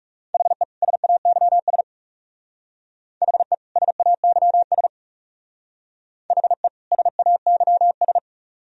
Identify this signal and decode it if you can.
Morse code, heard as a keyed tone, about 36 wpm, 700 Hz. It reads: HE SAYS HE SAYS HE SAYS
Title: Speed-Racing - Sets of 2 Words 30wpm